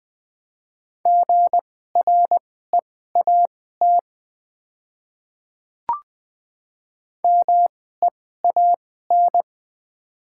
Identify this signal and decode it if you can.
Morse code, heard as a keyed tone, about 20 wpm, spaced as for 15 wpm, 700 Hz. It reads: GREAT MEAN
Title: 2x Character Spacing - Top 300 Words - 20wpm